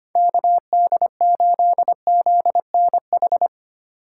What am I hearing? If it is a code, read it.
Morse code, heard as a keyed tone, about 32 wpm, 700 Hz. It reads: KD8ZNH